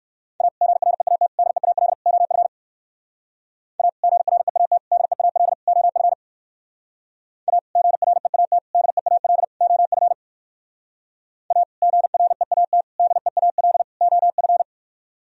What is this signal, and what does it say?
Morse code, heard as a keyed tone, about 60 wpm, 700 Hz. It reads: A GREAT DEAL OF A GREAT DEAL OF A GREAT DEAL OF A GREAT DEAL OF